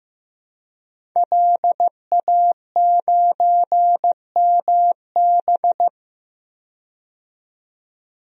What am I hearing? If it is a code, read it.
Morse code, heard as a keyed tone, about 15 wpm, 700 Hz. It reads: LA9MB
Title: DXCC-100 Call Signs - 5 Characters - 15wpm